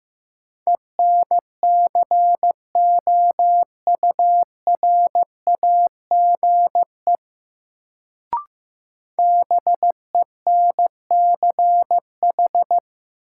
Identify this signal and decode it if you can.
Morse code, heard as a keyed tone, about 15 wpm, 700 Hz. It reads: ENCOURAGE BENCH
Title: Sentences from Top 1400 Words - New Word Drill - 15wpm